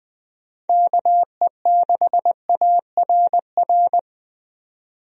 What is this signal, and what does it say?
Morse code, heard as a keyed tone, about 21 wpm, 700 Hz. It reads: KE6ARR
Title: USA Call Signs - 6 Characters - 20wpm